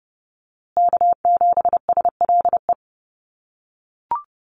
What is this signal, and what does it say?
Morse code, heard as a keyed tone, about 30 wpm, 700 Hz. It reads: K7SLE
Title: USA Call Signs Repeated 3x 30wpm